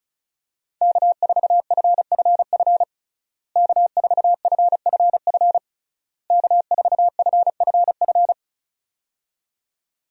Morse code, heard as a keyed tone, about 34 wpm, 700 Hz. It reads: K4FFF K4FFF K4FFF
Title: USA Call Signs - 5 Characters - Repeated 3x - 35wpm